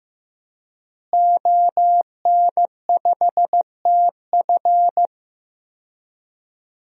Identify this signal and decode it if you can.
Morse code, heard as a keyed tone, about 15 wpm, 700 Hz. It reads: ON5TF
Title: DXCC-100 Call Signs - 5 Characters - Repeated 2x - 15wpm